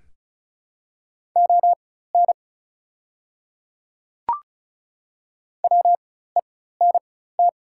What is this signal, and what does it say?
Morse code, heard as a keyed tone, about 33 wpm, spaced as for 16 wpm, 700 Hz. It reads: ON WENT
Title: 4x Character Spacing - Top 200 Words - 35wpm